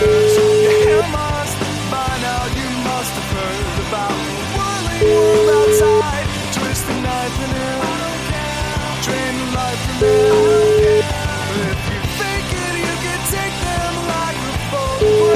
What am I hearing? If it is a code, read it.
Ringback tone (Europe, 425 Hz), with noise.